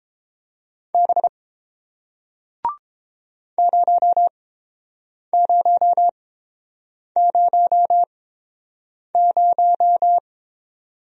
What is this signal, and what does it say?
Morse code, heard as a keyed tone, about 31 wpm, spaced as for 7 wpm, 700 Hz. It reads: B 0000